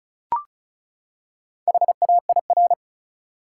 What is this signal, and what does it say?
Morse code, heard as a keyed tone, about 34 wpm, 700 Hz. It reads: HAIR
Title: Four Letter Words - 35wpm